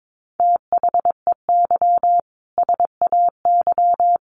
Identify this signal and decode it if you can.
Morse code, heard as a keyed tone, about 22 wpm, 700 Hz. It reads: THEY SAY
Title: Sentences from Top 1100 Words 22wpm